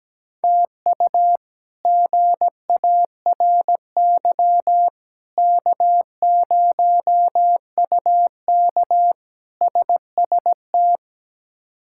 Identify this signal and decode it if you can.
Morse code, heard as a keyed tone, about 17 wpm, 700 Hz. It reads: TU GARY K0UK SST
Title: K1USN SST QSO Practice - 17wpm